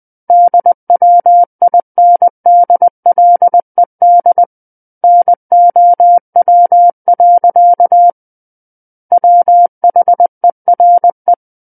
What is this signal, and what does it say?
Morse code, heard as a keyed tone, about 20 wpm, 700 Hz. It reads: DWINDLED NOW. WHERE
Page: SKCC Learning Center - Morse Code Practice Files